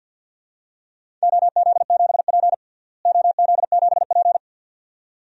Morse code, heard as a keyed tone, about 50 wpm, 700 Hz. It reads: OZ7P OZ7P
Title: DXCC-100 Call Signs - 4 Characters - Repeated 2x - 50wpm